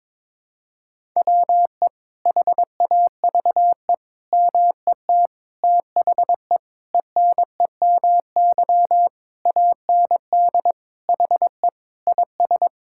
Morse code, heard as a keyed tone, about 22 wpm, 700 Hz. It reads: WE HAVE MET THE ENEMY AND HE IS
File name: Sets of 9 Words - 22wpm